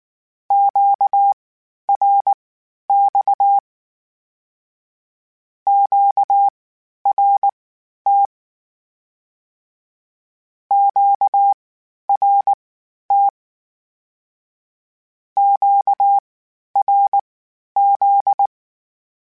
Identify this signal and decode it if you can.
Morse code, heard as a keyed tone, about 19 wpm, spaced as for 11 wpm, 800 Hz. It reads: QRX QRT QRT QRZ
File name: (spaced) / Q-Calls Spaced-mp3s / Q Calls Twice sp @18